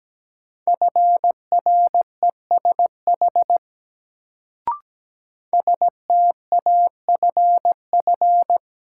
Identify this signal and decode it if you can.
Morse code, heard as a keyed tone, about 17 wpm, 700 Hz. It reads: FRESH STAFF
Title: Sentences from Top 1200 Words - New Word Drill Alt - 17wpm